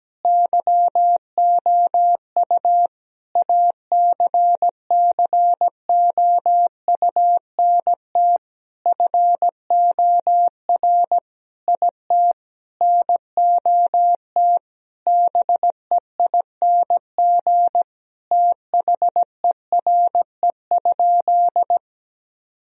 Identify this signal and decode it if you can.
Morse code, heard as a keyed tone, about 17 wpm, 700 Hz. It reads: YOU ACCOUNT FOR IT NOT BEING THERE?
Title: Sentences from Top 1000 Words - Review Part 3 - 17wpm